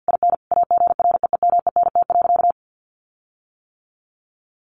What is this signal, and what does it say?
Morse code, heard as a keyed tone, about 50 wpm, 700 Hz. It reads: IN AGREEMENT.
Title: Sentences from Top 1500 Words 50wpm